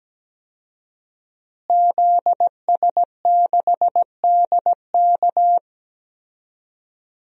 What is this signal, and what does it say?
Morse code, heard as a keyed tone, about 17 wpm, 700 Hz. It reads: ZS6DK
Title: DXCC-100 Call Signs - 5 Characters - Repeated 2x - 17wpm